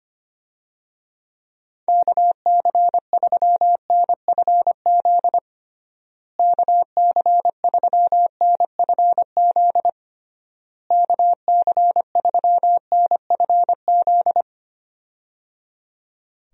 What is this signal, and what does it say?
Morse code, heard as a keyed tone, about 25 wpm, 700 Hz. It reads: KC3NFZ KC3NFZ KC3NFZ
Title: USA Call Signs - 6 Characters - Repeated 3x - 25wpm